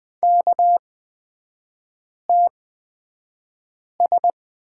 Morse code, heard as a keyed tone, about 20 wpm, spaced as for 5 wpm, 700 Hz. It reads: KTS
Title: 243 Lesson 39e - Letter z - Three Characters 20wpm